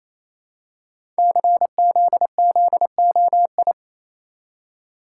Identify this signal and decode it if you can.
Morse code, heard as a keyed tone, about 27 wpm, 700 Hz. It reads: CZZOI
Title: Code Groups - 5 Letters - 28wpm